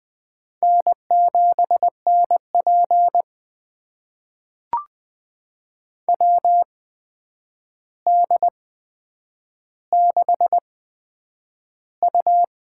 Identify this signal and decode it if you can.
Morse code, heard as a keyed tone, about 20 wpm, spaced as for 9 wpm, 700 Hz. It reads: N7NP WD6U
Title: USA Call Signs - 4 Characters - 8x Character Spacing - 20wpm